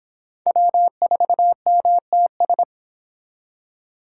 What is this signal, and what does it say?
Morse code, heard as a keyed tone, about 26 wpm, 700 Hz. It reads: W4MTS